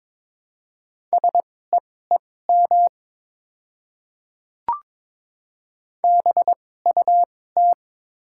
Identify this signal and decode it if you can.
Morse code, heard as a keyed tone, about 22 wpm, spaced as for 16 wpm, 700 Hz. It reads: SEEM BUT
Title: 2x Character Spacing - Top 300 Words - 22wpm